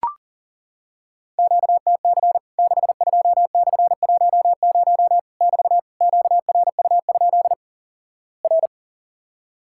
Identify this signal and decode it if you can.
Morse code, heard as a keyed tone, about 40 wpm, 690 Hz. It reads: QTC 62/10 <BT> QRU? R
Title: Worked All Europe DX Contest - QTC Exchange - 40wpm